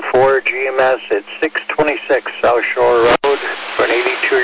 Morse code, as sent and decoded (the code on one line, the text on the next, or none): none